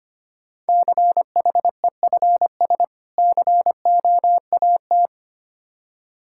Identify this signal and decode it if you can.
Morse code, heard as a keyed tone, about 25 wpm, 700 Hz. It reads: CHEFS COAT
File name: practice / 25wpm / Difficult Words - Speed-Racing with Hints - 25wpm